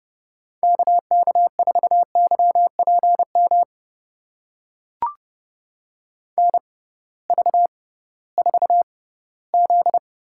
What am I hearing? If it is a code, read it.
Morse code, heard as a keyed tone, about 30 wpm, spaced as for 18 wpm, 700 Hz. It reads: KK4YPM NV4Z
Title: USA Call Signs - 6x Character Spacing - 30wpm